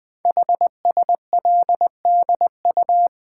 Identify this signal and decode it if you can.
Morse code, heard as a keyed tone, about 20 wpm, 700 Hz. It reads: HSLDU
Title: Code Groups - 5 Letter:Numbers - 20wpm